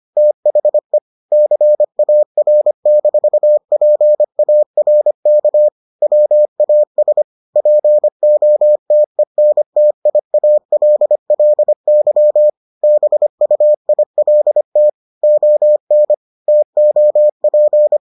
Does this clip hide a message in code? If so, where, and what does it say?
Yes: Morse code from 0.2 to 18.0 s, heard as a keyed tone, about 25 wpm, 600 Hz. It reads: THE CAR-PARK WAS POTENTIALLY BUILT ON TOP